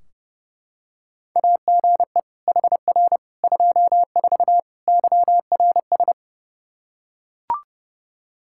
Morse code, heard as a keyed tone, about 31 wpm, 700 Hz. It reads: AGE HR 24 YRS
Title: Practice QSO - Exchange 3 - More Info - 30wpm